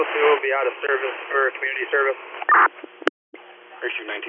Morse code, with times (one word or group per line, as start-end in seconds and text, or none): none